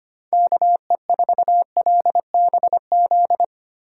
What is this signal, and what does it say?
Morse code, heard as a keyed tone, about 25 wpm, 700 Hz. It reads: KE4LBZ
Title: USA Call Signs - 6 Characters - 3x Character Spacing - 25wpm